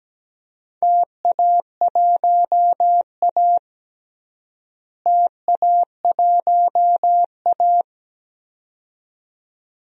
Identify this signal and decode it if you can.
Morse code, heard as a keyed tone, about 17 wpm, 700 Hz. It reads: TA1A TA1A
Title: DXCC-100 Call Signs - 4 Characters - Repeated 2x - 17wpm